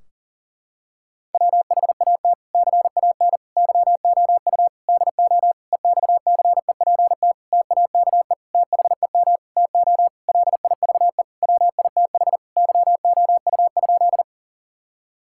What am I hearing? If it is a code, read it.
Morse code, heard as a keyed tone, about 40 wpm, 700 Hz. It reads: WHAT CAN YOU DO EXCEPT TAKE THEM TO LIVE WITH YOU?
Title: Sentences from Top 1600 Words 40wpm